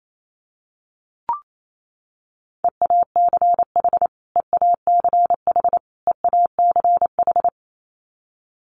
Morse code, heard as a keyed tone, about 28 wpm, 700 Hz. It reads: EACH EACH EACH
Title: Top 500 Words Repeated 28wpm